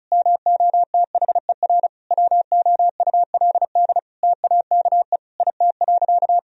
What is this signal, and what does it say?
Morse code, heard as a keyed tone, about 34 wpm, 700 Hz. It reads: MOTHER WOULD TAKE IT.
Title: Sentences from Top 1000 Words - Review Part 4 - 35wpm